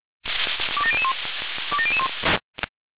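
Five-tone selective call (ZVEI), heard as quick runs of five tones, tones of about 80 ms. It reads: 27001 27001
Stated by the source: ZVEI five-tone call received by SDR.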